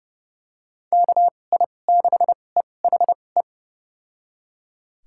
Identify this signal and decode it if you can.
Morse code, heard as a keyed tone, about 32 wpm, spaced as for 22 wpm, 700 Hz. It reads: KI6EHE
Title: USA Call Signs - 6 Characters - 2x Character Spacing - 30wpm